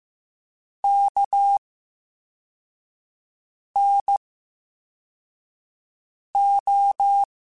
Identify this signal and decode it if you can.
Morse code, heard as a keyed tone, about 15 wpm, spaced as for 4 wpm, 800 Hz. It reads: KNO
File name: Special INGs / Special INGs-mp3s / Slow ing @13